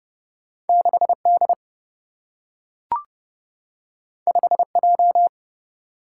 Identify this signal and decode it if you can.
Morse code, heard as a keyed tone, about 30 wpm, 700 Hz. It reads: 6D 5J
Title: Single-Letter Single-Number 30wpm